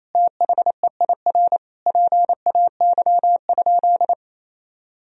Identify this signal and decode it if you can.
Morse code, heard as a keyed tone, about 28 wpm, 700 Hz. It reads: THEIR PAY?
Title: Sentences from Top 1000 Words - Review Part 4 - 28wpm